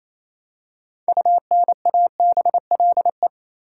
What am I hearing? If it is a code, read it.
Morse code, heard as a keyed tone, about 28 wpm, 700 Hz. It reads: UNABLE
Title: Six Letter Words - 28wpm